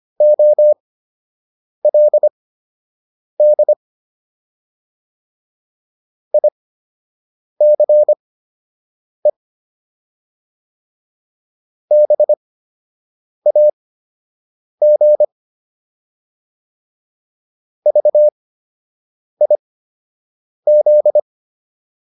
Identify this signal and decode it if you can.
Morse code, heard as a keyed tone, about 25 wpm, spaced as for 7 wpm, 600 Hz. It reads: OLD ICE BAG VIZ